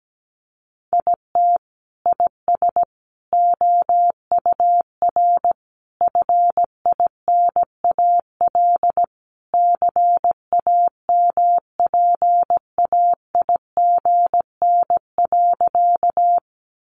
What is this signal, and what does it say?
Morse code, heard as a keyed tone, about 17 wpm, 700 Hz. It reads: IT IS OUR FINAL CAMPAIGN.
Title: Sentences from Top 1200 Words 17wpm